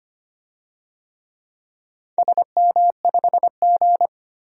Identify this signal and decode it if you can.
Morse code, heard as a keyed tone, about 25 wpm, 700 Hz. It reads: SM5G
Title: DXCC-100 Call Signs - 4 Characters - 25wpm